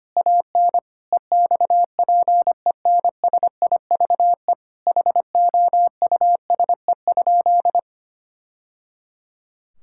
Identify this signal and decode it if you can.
Morse code, heard as a keyed tone, about 25 wpm, 700 Hz. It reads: AN EXPENSIVE HOUSE?
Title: Sentences from Top 1000 Words - Review Part 6 - 25wpm